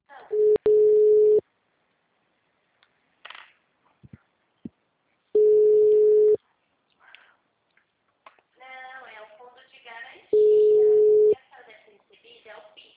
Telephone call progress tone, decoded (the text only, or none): ringback tone